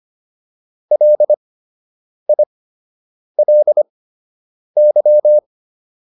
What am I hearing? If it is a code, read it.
Morse code, heard as a keyed tone, about 25 wpm, spaced as for 8 wpm, 600 Hz. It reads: LILY